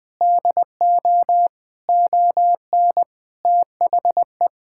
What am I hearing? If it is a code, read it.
Morse code, heard as a keyed tone, about 20 wpm, 700 Hz. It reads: DO ON THE
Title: Sentences with 9-Word Phrases - 20wpm